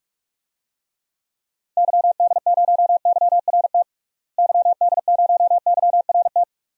Morse code, heard as a keyed tone, about 42 wpm, 700 Hz. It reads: YD0YRT YD0YRT
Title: DXCC-100 Call Signs - 6 Characters - Repeated 2x - 45wpm